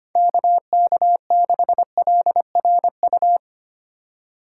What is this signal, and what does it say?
Morse code, heard as a keyed tone, about 25 wpm, 700 Hz. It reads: KK6LRU
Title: USA Call Signs - 6 Characters - 4x Character Spacing - 25wpm